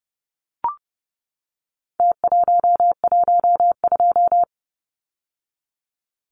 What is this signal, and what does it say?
Morse code, heard as a keyed tone, about 30 wpm, 700 Hz. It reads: T112
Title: Numbers - Contest Style - 4-Digit 30wpm